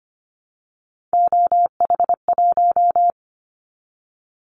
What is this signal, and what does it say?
Morse code, heard as a keyed tone, about 25 wpm, 700 Hz. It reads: OH1
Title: Callsign Prefixes for Top 50 DXCC - 25wpm